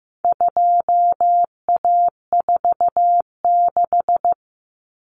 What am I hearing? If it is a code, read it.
Morse code, heard as a keyed tone, about 15 wpm, 700 Hz. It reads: 2A46
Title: Common Cut Numbers - 4-Digit 15wpm